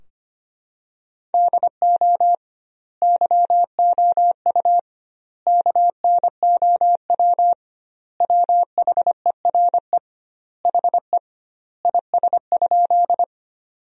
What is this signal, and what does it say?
Morse code, heard as a keyed tone, about 25 wpm, 700 Hz. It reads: DO YOU KNOW WHERE HE IS?
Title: 2x Word-Spacing - Sentences from Top 300 Words - 25wpm